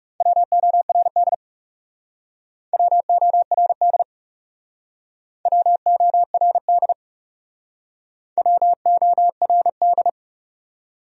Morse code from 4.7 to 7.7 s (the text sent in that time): WORD